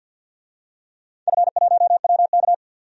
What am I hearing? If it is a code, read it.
Morse code, heard as a keyed tone, about 51 wpm, 700 Hz. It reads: F1WK
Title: DXCC-100 Call Signs - 4 Characters - 50wpm